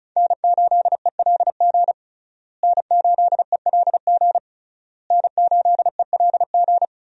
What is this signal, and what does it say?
Morse code, heard as a keyed tone, about 36 wpm, 700 Hz. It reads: N8ELG N8ELG N8ELG
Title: USA Call Signs - 5 Characters - Repeated 3x - 35wpm